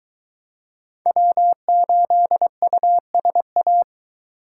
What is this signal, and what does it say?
Morse code, heard as a keyed tone, about 22 wpm, 700 Hz. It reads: W8USA